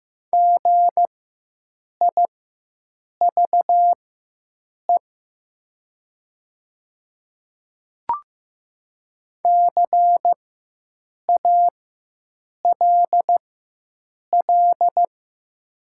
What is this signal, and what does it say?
Morse code, heard as a keyed tone, about 15 wpm, spaced as for 7 wpm, 700 Hz. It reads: GIVE CALL
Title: 4x Character Spacing - Top 200 Words - 15wpm